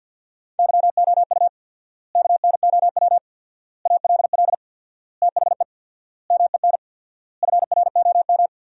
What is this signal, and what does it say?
Morse code, heard as a keyed tone, about 48 wpm, 700 Hz. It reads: YOU KNOW ALL THE MEN FROM
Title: 4x Word-Spacing - Sentences from Top 300 Words - 50wpm